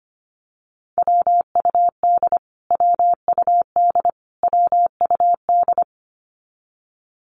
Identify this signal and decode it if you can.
Morse code, heard as a keyed tone, about 25 wpm, 700 Hz. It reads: WUD WUD WUD
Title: Abbreviations Repeated 3x - 25wpm